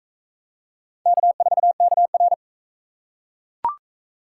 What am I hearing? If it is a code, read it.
Morse code, heard as a keyed tone, about 39 wpm, 700 Hz. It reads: K4KR